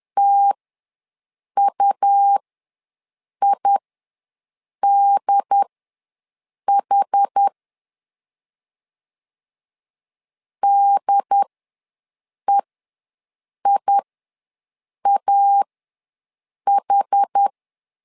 Morse code, heard as a keyed tone, about 10 wpm, spaced as for 6 wpm, 800 Hz. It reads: TUIDH DEIAH
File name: LEZIONI / 2 UHDN / 2 UHD ws 10wpm 2058